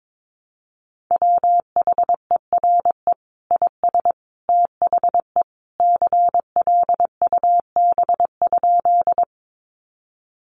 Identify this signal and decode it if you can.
Morse code, heard as a keyed tone, about 22 wpm, 700 Hz. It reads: WHERE IS THE CLUB?
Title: Sentences from Top 1200 Words 22wpm